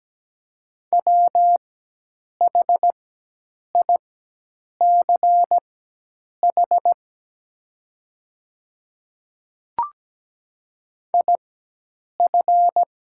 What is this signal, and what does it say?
Morse code, heard as a keyed tone, about 17 wpm, spaced as for 8 wpm, 700 Hz. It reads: WHICH IF
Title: 4x Character Spacing - Top 100 Words - 17wpm